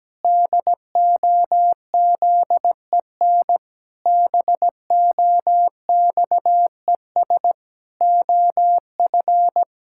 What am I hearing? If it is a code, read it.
Morse code, heard as a keyed tone, about 17 wpm, 700 Hz. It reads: DOZEN BOXES OF
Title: Pangram Challenge v2 - 17wpm